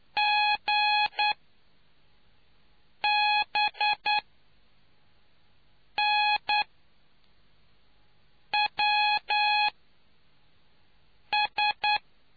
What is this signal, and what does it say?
Morse code, heard as a keyed tone, about 9 wpm, spaced as for 4 wpm, 840 Hz. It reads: GBNWS